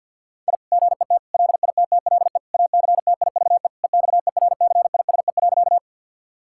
Morse code, heard as a keyed tone, about 50 wpm, 700 Hz. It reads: I GET LITTLE ACTIVE EXERCISE.
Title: Sentences from Top 1000 Words - Review Part 6 - 50wpm